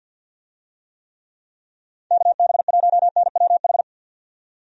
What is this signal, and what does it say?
Morse code, heard as a keyed tone, about 51 wpm, 700 Hz. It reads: KB1NWH
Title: USA Call Signs - 6 Characters - 50wpm